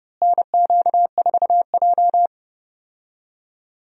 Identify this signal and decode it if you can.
Morse code, heard as a keyed tone, about 30 wpm, 700 Hz. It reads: NQ4J
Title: USA Call Signs - 4 Characters - Repeated 3x - 30wpm